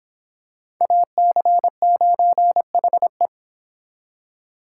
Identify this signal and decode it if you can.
Morse code, heard as a keyed tone, about 26 wpm, 700 Hz. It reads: AC9HE